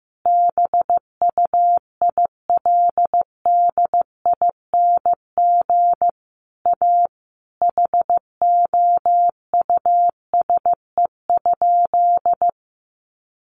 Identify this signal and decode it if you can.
Morse code, heard as a keyed tone, about 15 wpm, 700 Hz. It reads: BUILDING A HOUSE?
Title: Sentences from Top 700 Words 15wpm